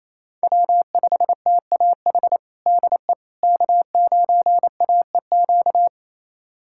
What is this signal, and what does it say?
Morse code, heard as a keyed tone, about 28 wpm, 700 Hz. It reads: W5TAH DE K9AEQ
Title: USA Callsign DE Callsign - 5 Characters - 28wpm